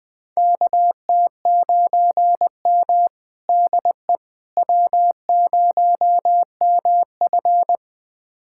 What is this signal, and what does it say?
Morse code, heard as a keyed tone, about 20 wpm, 700 Hz. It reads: KT9M DE W0MF
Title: USA Callsign DE Callsign - 4 Characters - 20wpm